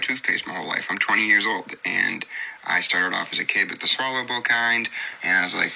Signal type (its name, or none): Emergency Alert System (SAME) tones